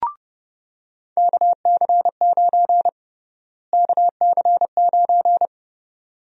Morse code, heard as a keyed tone, about 31 wpm, 700 Hz. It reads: KC9 KC9
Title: USA Call Sign Prefixes - Repeated 2x - 30wpm